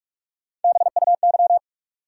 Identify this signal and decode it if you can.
Morse code, heard as a keyed tone, about 47 wpm, 700 Hz. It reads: BUY
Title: Sentences from Top 700 Words - New Word Drill - 45wpm